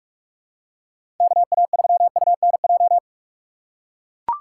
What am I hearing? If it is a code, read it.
Morse code, heard as a keyed tone, about 44 wpm, 700 Hz. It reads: KA3UNJ